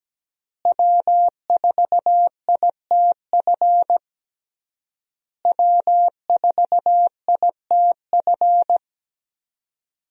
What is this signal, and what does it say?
Morse code, heard as a keyed tone, about 17 wpm, 700 Hz. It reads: W4ITF W4ITF
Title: USA Call Signs - 5 Characters - Repeated 3x - 17wpm